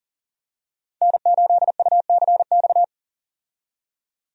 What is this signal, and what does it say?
Morse code, heard as a keyed tone, about 40 wpm, 700 Hz. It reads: N8UCX
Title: USA Call Signs - 5 Characters - 40wpm